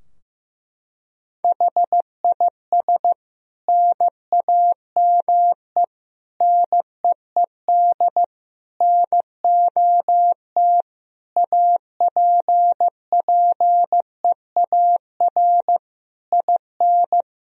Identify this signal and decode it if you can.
Morse code, heard as a keyed tone, about 15 wpm, 700 Hz. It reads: HIS NAME NEED NOT APPEAR IN
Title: Sentences from Top 1000 Words - Review Part 2 - 15wpm